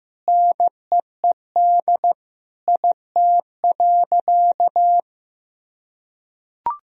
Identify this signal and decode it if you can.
Morse code, heard as a keyed tone, about 15 wpm, 700 Hz. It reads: NEED IT.
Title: Sentences from Top 1000 Words - Review Part 4 - 15wpm